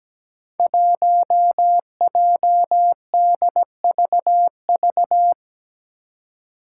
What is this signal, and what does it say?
Morse code, heard as a keyed tone, about 17 wpm, 700 Hz. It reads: 1JDVV
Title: Five Challenging Letter-Number Combinations 17wpm